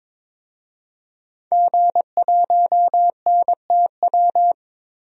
Morse code, heard as a keyed tone, about 22 wpm, 700 Hz. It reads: G1NTW
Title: DXCC-100 Call Signs - 5 Characters - 22wpm